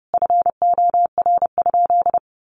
Morse code, heard as a keyed tone, about 28 wpm, 700 Hz. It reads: FOR?
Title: Sentences from Top 100 Words 30wpm